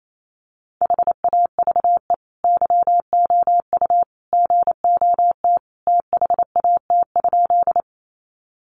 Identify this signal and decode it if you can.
Morse code, heard as a keyed tone, about 28 wpm, 700 Hz. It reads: HAVE YOU GOT THAT?
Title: Sentences from Top 300 Words 28wpm